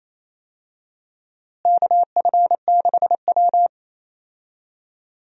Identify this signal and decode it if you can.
Morse code, heard as a keyed tone, about 28 wpm, 700 Hz. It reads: KF6W
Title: USA Call Signs - 4 Characters - Repeated 2x - 28wpm